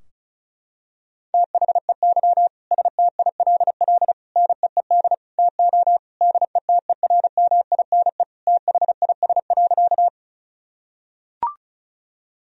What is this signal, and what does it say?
Morse code, heard as a keyed tone, about 36 wpm, 700 Hz. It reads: THEY STILL NEED TO DETERMINE THIS.
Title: Sentences from Top 1000 Words - Review Part 4 - 35wpm